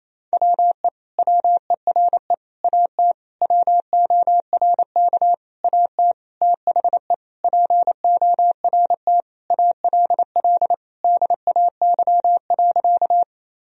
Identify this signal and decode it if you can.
Morse code, heard as a keyed tone, about 28 wpm, 700 Hz. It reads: WE WERE AT WORK AT THE PORT ALL DAY.
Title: Sentences from Top 1000 Words - Review Part 1 - 28wpm